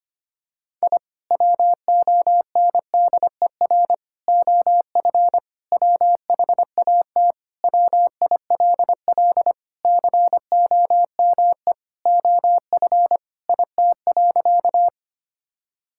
Morse code, heard as a keyed tone, about 25 wpm, 700 Hz. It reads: I WONDER OF WHAT WILL COME OF IT.
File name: practice / 25wpm / Sentences with 2-Word Phrases - 25wpm